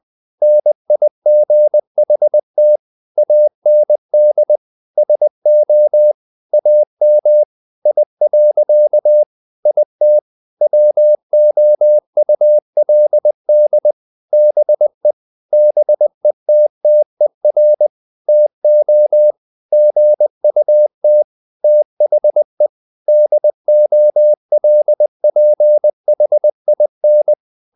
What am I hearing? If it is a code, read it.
Morse code, heard as a keyed tone, about 20 wpm, 600 Hz. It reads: NIGHT AND SO AM I. IT WOULD BE BETTER TO GUT THE DOLPHIN
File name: the old man and the sea part 2 20wpm